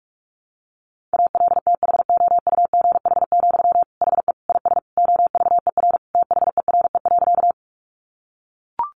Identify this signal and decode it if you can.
Morse code, heard as a keyed tone, about 44 wpm, 700 Hz. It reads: ALTHOUGH, HE IS OVER THERE.
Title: Sentences from Top 600 Words 45wpm